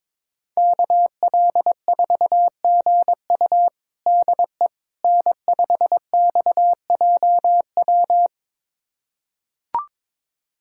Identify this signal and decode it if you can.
Morse code, heard as a keyed tone, about 22 wpm, 700 Hz. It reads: KL4GU DE N5XJW
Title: USA Callsign DE Callsign - 5 Characters - 22wpm